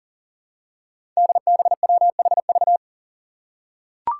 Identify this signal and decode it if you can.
Morse code, heard as a keyed tone, about 39 wpm, 700 Hz. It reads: DBWHV
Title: Five Challenging Letter-Number Combinations 40wpm